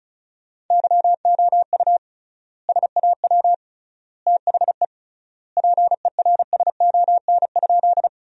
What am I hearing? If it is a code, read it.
Morse code, heard as a keyed tone, about 36 wpm, 700 Hz. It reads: YOU SAW THE PERSON?
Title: 3x Word-Spacing - Sentences from Top 400 Words - 35wpm